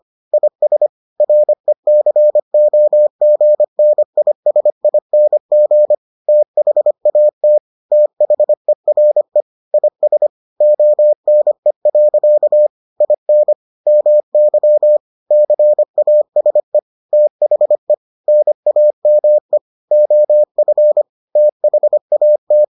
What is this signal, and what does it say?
Morse code, heard as a keyed tone, about 25 wpm, 600 Hz. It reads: IS RECOGNISING THAT THERE IS ONE. IN MY CASE THE NAME OF THAT